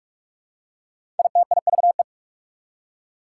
Morse code, heard as a keyed tone, about 46 wpm, 690 Hz. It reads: ITIVE
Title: Prefixes and Suffixes 45wpm